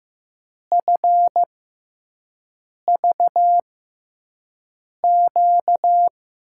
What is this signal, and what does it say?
Morse code, heard as a keyed tone, about 15 wpm, spaced as for 5 wpm, 700 Hz. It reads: FVQ
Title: USA Call Signs - 6x Character Spacing - 15wpm